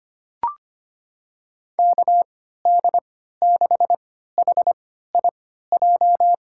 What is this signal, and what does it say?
Morse code, heard as a keyed tone, about 25 wpm, spaced as for 14 wpm, 700 Hz. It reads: KD6HIJ
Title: USA Call Signs - 6 Characters - 3x Character Spacing - 25wpm